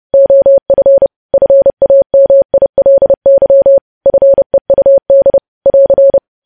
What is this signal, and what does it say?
Morse code, heard as a keyed tone, about 30 wpm, 550 Hz. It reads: OF FAMILY FEUD <AR>
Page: news headlines in morse code at 30 wpm